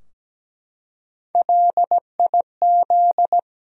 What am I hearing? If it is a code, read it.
Morse code, heard as a keyed tone, about 17 wpm, 700 Hz. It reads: LIZ
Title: USA Common Names - Max 3 Letters - 17wpm